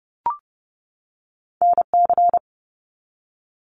Morse code, heard as a keyed tone, about 30 wpm, 700 Hz. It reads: NC
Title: US States 30wpm